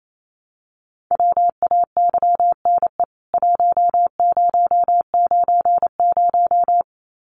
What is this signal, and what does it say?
Morse code, heard as a keyed tone, about 28 wpm, 700 Hz. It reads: WAYNE 1090
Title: CWT Exchange 28wpm